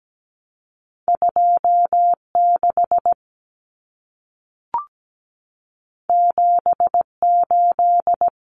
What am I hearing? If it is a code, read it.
Morse code, heard as a keyed tone, about 17 wpm, 700 Hz. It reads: 26 78
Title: Common Cut Numbers - 2-Digit 17wpm